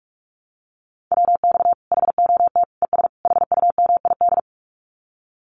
Morse code, heard as a keyed tone, about 44 wpm, 700 Hz. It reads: WX HOT ES HUMID
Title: Common QSO Phrases 45wpm